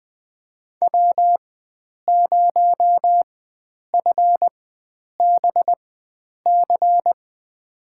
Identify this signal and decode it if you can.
Morse code, heard as a keyed tone, about 20 wpm, spaced as for 9 wpm, 700 Hz. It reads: W0FBC
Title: USA Call Signs - 4x Character Spacing - 20wpm